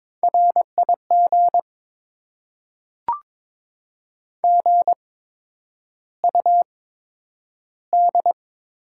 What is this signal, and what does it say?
Morse code, heard as a keyed tone, about 22 wpm, spaced as for 9 wpm, 700 Hz. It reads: RIG GUD
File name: Top 20 QSO Elements - 8x Character Spacing - 22wpm